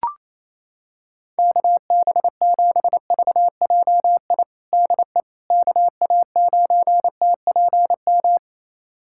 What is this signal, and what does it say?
Morse code, heard as a keyed tone, about 28 wpm, 700 Hz. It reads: KB7VJI DE KA9TPM
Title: USA Callsign DE Callsign - 6 Characters - 28wpm